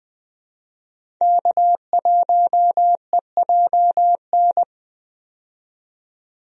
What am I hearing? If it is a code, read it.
Morse code, heard as a keyed tone, about 20 wpm, 700 Hz. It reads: K1EJN
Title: USA Call Signs - 5 Characters - 20wpm